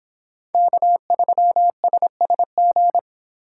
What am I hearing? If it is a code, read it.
Morse code, heard as a keyed tone, about 26 wpm, 700 Hz. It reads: K3SSG